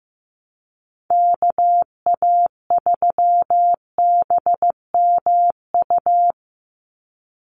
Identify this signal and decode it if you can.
Morse code, heard as a keyed tone, about 15 wpm, 700 Hz. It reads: KA3BMU
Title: Call Signs Repeated 2x - 15wpm